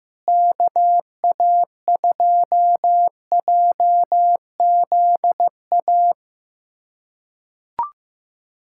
Morse code, heard as a keyed tone, about 15 wpm, 700 Hz. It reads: KA2JZA
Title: USA Callsign DE Callsign 15wpm